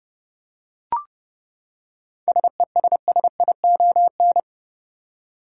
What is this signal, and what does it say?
Morse code, heard as a keyed tone, about 30 wpm, 700 Hz. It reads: SESSION
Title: Seven Letter Words - 30wpm